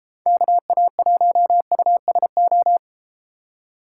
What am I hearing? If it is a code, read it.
Morse code, heard as a keyed tone, about 31 wpm, 700 Hz. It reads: KA1USO